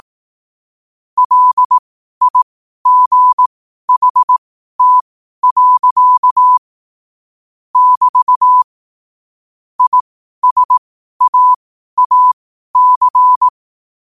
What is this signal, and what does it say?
Morse code, heard as a keyed tone, about 18 wpm, spaced as for 12 wpm, 1000 Hz. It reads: LIGHT. <BT> ISAAC
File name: qotdfiles / 12wpm